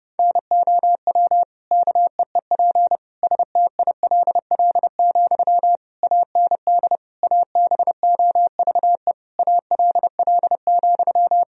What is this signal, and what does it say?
Morse code, heard as a keyed tone, about 30 wpm, 700 Hz. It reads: NOW KEEP STILL, AND ABOVE ALL,
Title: Sentences from Top 1000 Words - Review Part 2 - 30wpm